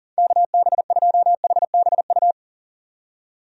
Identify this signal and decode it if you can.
Morse code, heard as a keyed tone, about 40 wpm, 700 Hz. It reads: KB2HBU
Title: USA Call Signs - 6 Characters - 40wpm